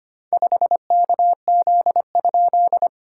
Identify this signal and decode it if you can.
Morse code, heard as a keyed tone, about 25 wpm, 690 Hz. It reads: 5KZ?
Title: SOTA QSO Practice 25wpm